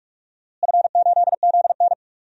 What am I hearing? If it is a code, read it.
Morse code, heard as a keyed tone, about 46 wpm, 700 Hz. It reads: F8ZN
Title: DXCC-100 Call Signs - 4 Characters - Repeated 3x - 45wpm